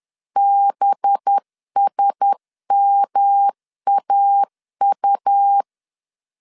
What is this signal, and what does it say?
Morse code, heard as a keyed tone, about 10 wpm, 800 Hz. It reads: BSMAU
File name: LEZIONI / 7 GF9Z / 7 GF 10wpm 2514